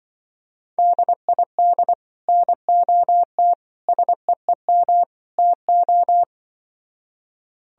Morse code, heard as a keyed tone, about 24 wpm, 700 Hz. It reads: DID NOT SEEM TO
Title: Speed-Racing - Sets of 4 Words 20wpm